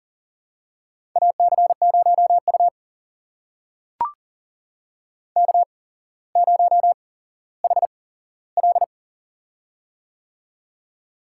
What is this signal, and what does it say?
Morse code, heard as a keyed tone, about 42 wpm, spaced as for 17 wpm, 690 Hz. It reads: AC0U K0HL
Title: USA Call Signs - 4 Characters - 8x Character Spacing - 40wpm